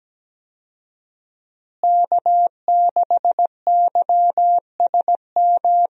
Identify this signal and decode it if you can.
Morse code, heard as a keyed tone, about 17 wpm, 700 Hz. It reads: K6YSM
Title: USA Call Signs - 5 Characters - Repeated 3x - 17wpm